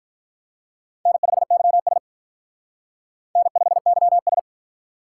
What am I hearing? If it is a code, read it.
Morse code, heard as a keyed tone, about 45 wpm, 700 Hz. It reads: N5YS N5YS